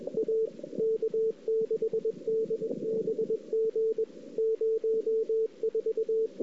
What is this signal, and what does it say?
Morse code, heard as a keyed tone, about 29 wpm, spaced as for 26 wpm, 410 Hz. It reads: A K6DBG 04
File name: CW / 2008-ss-cw-k5tr-op-n5rz -05:00